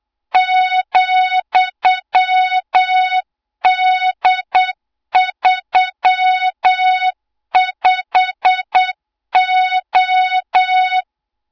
Morse code, heard as a keyed tone, about 8 wpm, 3700 Hz, with noise. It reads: ,D35O
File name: Morze-HA2MN / hang / 20 GYAKORLAT